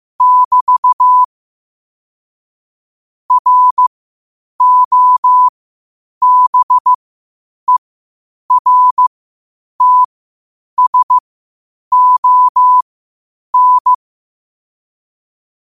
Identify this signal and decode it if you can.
Morse code, heard as a keyed tone, about 15 wpm, spaced as for 8 wpm, 1000 Hz. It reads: <BT> ROBERTSON
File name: qotdfiles / 8wpm